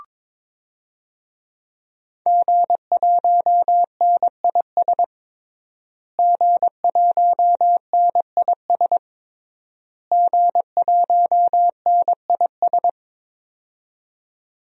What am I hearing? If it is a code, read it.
Morse code, heard as a keyed tone, about 22 wpm, 700 Hz. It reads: G1NIS G1NIS G1NIS
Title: DXCC-100 Call Signs - 5 Characters - Repeated 3x - 22wpm